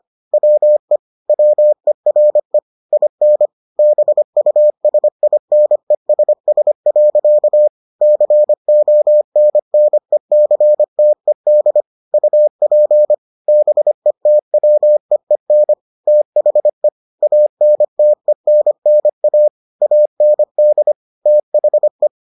Morse code, heard as a keyed tone, about 25 wpm, 600 Hz. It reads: WE WERE IN BUSINESS. CONNECTED UP BETWEEN THE ANTENNA AND THE